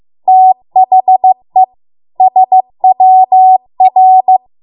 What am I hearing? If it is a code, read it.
Morse code, heard as a keyed tone, about 15 wpm, 750 Hz. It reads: THE SWR